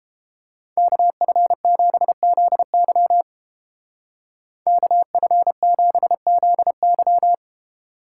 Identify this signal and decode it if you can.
Morse code, heard as a keyed tone, about 30 wpm, 700 Hz. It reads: KF7ZY KF7ZY